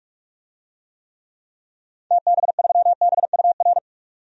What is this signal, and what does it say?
Morse code, heard as a keyed tone, about 44 wpm, 700 Hz. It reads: TB3BUR
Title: DXCC-100 Call Signs - 6 Characters - 45wpm